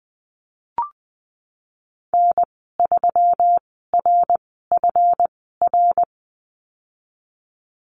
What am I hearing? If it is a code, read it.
Morse code, heard as a keyed tone, about 20 wpm, spaced as for 15 wpm, 700 Hz. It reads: N3RFR
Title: 140 Lesson 23g - Letter f - Call Signs 20wpm